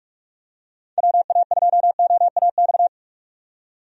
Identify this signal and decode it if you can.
Morse code, heard as a keyed tone, about 46 wpm, 700 Hz. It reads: WA2OAX
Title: USA Call Signs - 6 Characters - 45wpm